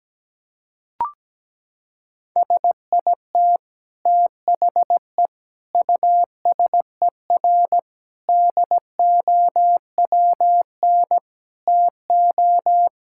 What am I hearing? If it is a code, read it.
Morse code, heard as a keyed tone, about 17 wpm, 700 Hz. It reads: SIT THE USER DOWN TO
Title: Sentences from Top 1000 Words - Review Part 5 - 17wpm